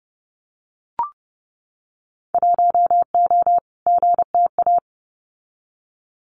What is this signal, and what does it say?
Morse code, heard as a keyed tone, about 30 wpm, 700 Hz. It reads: 1O GTA
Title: Winter Field Day Exchanges 30wpm